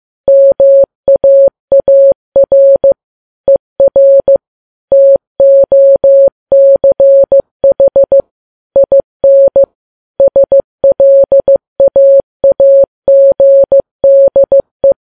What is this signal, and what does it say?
Morse code, heard as a keyed tone, about 15 wpm, 550 Hz. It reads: MAAR ER TOCH IN SLAAGDE
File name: Files Web tls / Morse on9fv / morse15kps